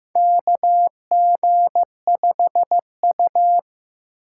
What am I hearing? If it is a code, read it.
Morse code, heard as a keyed tone, about 15 wpm, 700 Hz. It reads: KG5U
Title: USA Call Signs - 6x Character Spacing - 15wpm